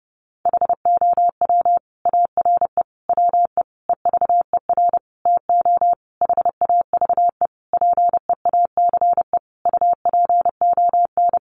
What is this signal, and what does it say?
Morse code, heard as a keyed tone, about 30 wpm, 700 Hz. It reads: HOW ARE WE EVER TO HAVE PEACE UPON
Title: Sentences from Top 1500 Words 30wpm